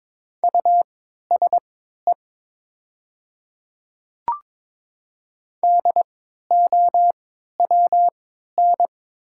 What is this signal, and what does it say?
Morse code, heard as a keyed tone, about 22 wpm, spaced as for 12 wpm, 700 Hz. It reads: USE DOWN
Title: 3x Character Spacing - Top 300 Words - 22wpm